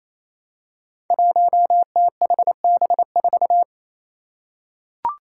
Morse code, heard as a keyed tone, about 28 wpm, 700 Hz. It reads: 1THB4
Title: Five Challenging Letter-Number Combinations 28wpm